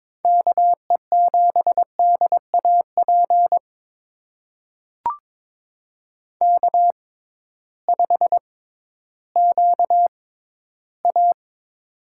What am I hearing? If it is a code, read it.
Morse code, heard as a keyed tone, about 22 wpm, spaced as for 13 wpm, 700 Hz. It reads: KE7DAP K5QA
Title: USA Call Signs - 6x Character Spacing - 22wpm